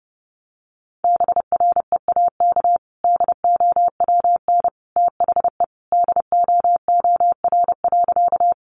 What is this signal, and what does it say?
Morse code, heard as a keyed tone, about 30 wpm, 700 Hz. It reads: BREAK DOWN THE DOOR.
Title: Sentences from Top 900 Words 30wpm